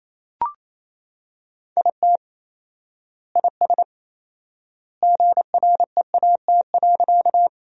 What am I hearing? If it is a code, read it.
Morse code, heard as a keyed tone, about 28 wpm, 700 Hz. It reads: IT IS GREAT.
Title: 4x Word-Spacing - Sentences from Top 200 Words - 28wpm